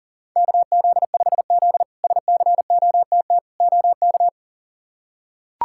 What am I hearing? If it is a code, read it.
Morse code, heard as a keyed tone, about 40 wpm, 700 Hz. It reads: KZ5Z SCOTT OK
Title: K1USN SST QSO Practice - 40wpm